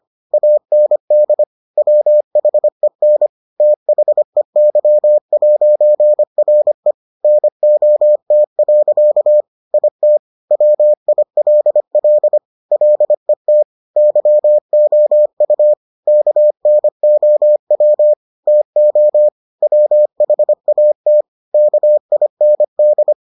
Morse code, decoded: AND WHEN THEY'RE NOT. IT WILL LET YOU KNOW TO WHAT KIND

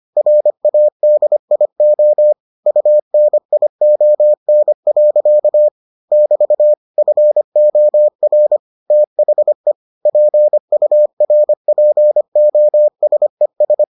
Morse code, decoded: RADIO UNION. <BT> FOR THE PURPOSES

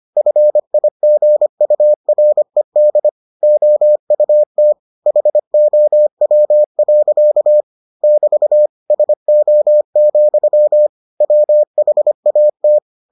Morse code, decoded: FIGURED OUT HOW. <BT> SO, WHAT

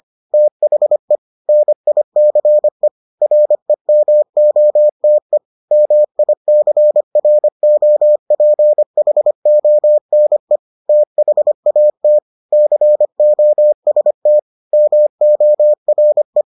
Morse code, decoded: THE NICE REMOTE MICROPHONE THAT COST MORE

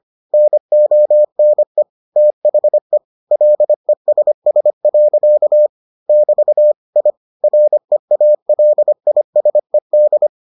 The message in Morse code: NONE THE LESS. <BT> I REALISED

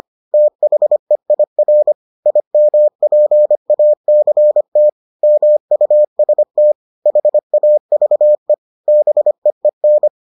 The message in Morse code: THEIR IMPACT MUST HAVE BEEN

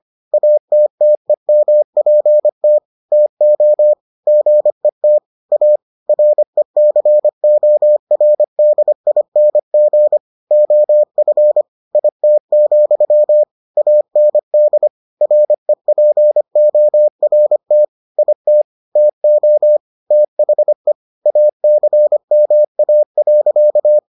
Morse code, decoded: ATTEMPT TO GET A RECORDING OF IT, AND REPORT IT TO THE ACMA.

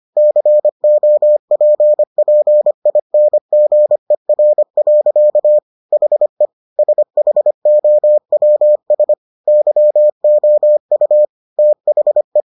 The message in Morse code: COPPINGER. HE SHOWS YOU THE

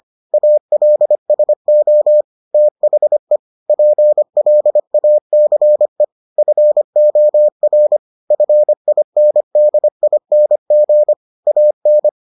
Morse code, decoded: ALSO THE PLACE FOR FINDING AN

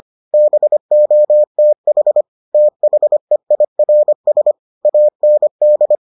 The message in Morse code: BOTH THEIRS AND

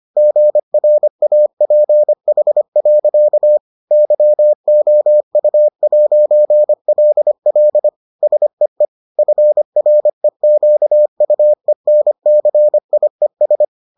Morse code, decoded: GRAPH. YOU'LL SEE FREQUENCIES

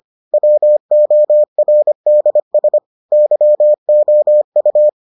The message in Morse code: WORDS YOU

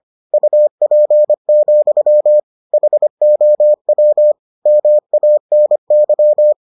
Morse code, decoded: UP, HOW MANY

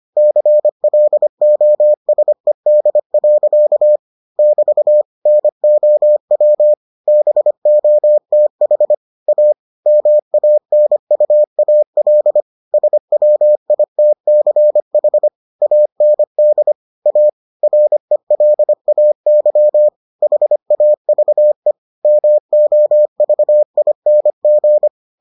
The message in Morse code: CLOSED. <BT> NOW BOTH A MANUAL SWITCH AND A RELAY HAVE MOVING

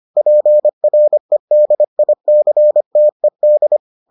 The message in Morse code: PREDICTED

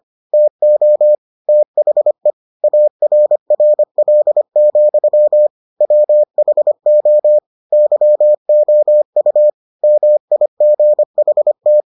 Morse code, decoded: TO THE ARRL, WHO YOU MIGHT

